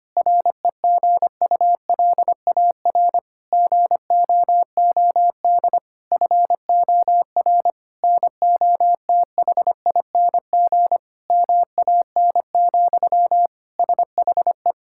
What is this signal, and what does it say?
Morse code, heard as a keyed tone, about 25 wpm, 700 Hz. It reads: REGULAR GOOD FOR NOTHING MAN, SHE
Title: Sentences from Top 1000 Words - Review Part 6 - 25wpm